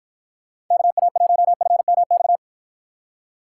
Morse code, heard as a keyed tone, about 53 wpm, 700 Hz. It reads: KA1FMX